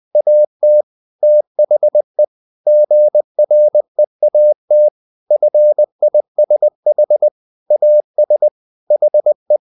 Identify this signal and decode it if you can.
Morse code, heard as a keyed tone, about 20 wpm, 600 Hz. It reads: AT THE GREAT FISH AS HE
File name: the old man and the sea part 2 20wpm